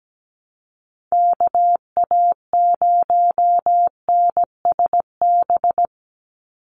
Morse code, heard as a keyed tone, about 17 wpm, 700 Hz. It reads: KA0NSB
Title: USA Call Signs 17wpm